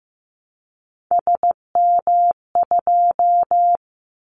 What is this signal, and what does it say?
Morse code, heard as a keyed tone, about 15 wpm, 700 Hz. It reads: SM2
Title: Callsign Prefixes for Top 25 DXCC - Repeated 3x - 15wpm